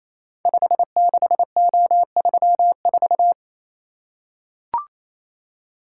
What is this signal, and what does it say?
Morse code, heard as a keyed tone, about 28 wpm, 700 Hz. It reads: 56O34
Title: Code Groups - 5 Letter:Numbers - 28wpm